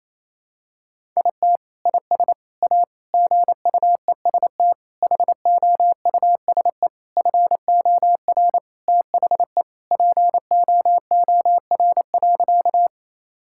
Morse code, heard as a keyed tone, about 27 wpm, 700 Hz. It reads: IT IS A GUEST HOUSE FOR THE POOR.
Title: Sentences from Top 1600 Words 28wpm